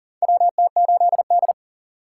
Morse code, heard as a keyed tone, about 40 wpm, 700 Hz. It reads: WT8D
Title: USA Call Signs - 4x Character Spacing - 40wpm